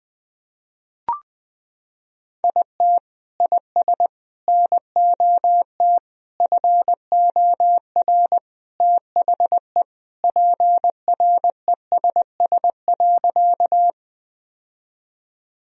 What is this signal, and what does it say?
Morse code, heard as a keyed tone, about 20 wpm, 700 Hz. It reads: IT IS NOT FOR THE PRESS.
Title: Sentences from Top 1000 Words - Review Part 1 - 20wpm